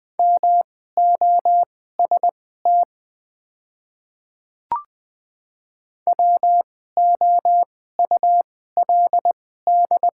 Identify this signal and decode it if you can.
Morse code, heard as a keyed tone, about 20 wpm, spaced as for 14 wpm, 700 Hz. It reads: MOST WOULD
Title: 2x Character Spacing - Top 500 Words - 20wpm